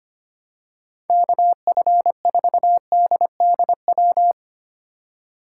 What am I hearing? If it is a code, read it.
Morse code, heard as a keyed tone, about 25 wpm, 700 Hz. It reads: KF4DDW
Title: USA Call Signs - 6 Characters - 4x Character Spacing - 25wpm